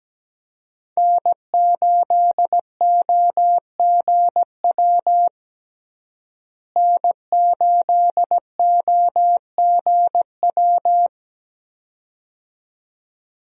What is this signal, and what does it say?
Morse code, heard as a keyed tone, about 17 wpm, 700 Hz. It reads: N8OGW N8OGW
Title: USA Call Signs - 5 Characters - Repeated 2x - 17wpm